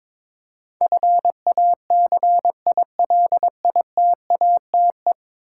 Morse code, heard as a keyed tone, about 22 wpm, 700 Hz. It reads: FACILITATE
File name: practice / 22wpm / 10 Letter Words - First 3 Letters Given - 22wpm